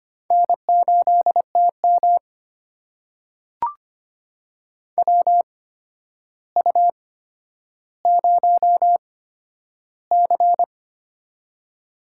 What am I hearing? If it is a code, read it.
Morse code, heard as a keyed tone, about 25 wpm, spaced as for 11 wpm, 700 Hz. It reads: N8TM WU0C
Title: USA Call Signs - 4 Characters - 8x Character Spacing - 25wpm